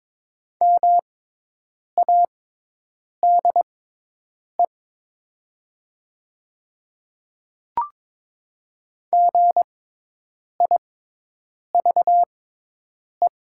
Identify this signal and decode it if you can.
Morse code, heard as a keyed tone, about 21 wpm, spaced as for 8 wpm, 700 Hz. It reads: MADE GIVE
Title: 6x Character Spacing - Top 300 Words - 22wpm